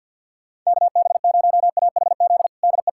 Morse code, heard as a keyed tone, about 51 wpm, 700 Hz. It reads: KB0AHZ DE
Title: USA Callsign DE Callsign - 6 Characters - 50wpm